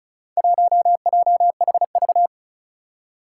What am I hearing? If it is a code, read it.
Morse code, heard as a keyed tone, about 36 wpm, 700 Hz. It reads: 1JHV
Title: Four Challenging Letter-Number Combinations 35wpm